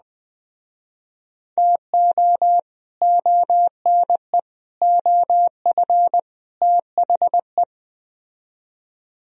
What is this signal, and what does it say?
Morse code, heard as a keyed tone, about 20 wpm, 700 Hz. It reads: TO ONE OF THE